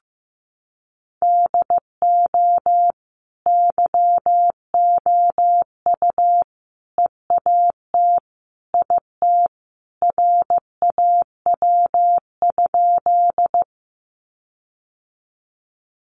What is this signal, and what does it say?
Morse code, heard as a keyed tone, about 15 wpm, 700 Hz. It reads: DO YOU EAT IT RAW?
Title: Sentences from Top 1500 Words 15wpm